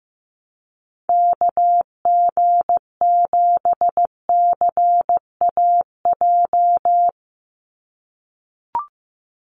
Morse code, heard as a keyed tone, about 15 wpm, 700 Hz. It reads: KG7CAJ
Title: USA Call Signs 15wpm